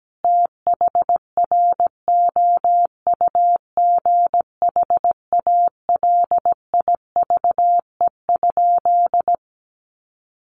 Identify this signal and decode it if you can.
Morse code, heard as a keyed tone, about 17 wpm, 700 Hz. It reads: THROUGH ALIVE?
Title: A Sound Of Thunder 17wpm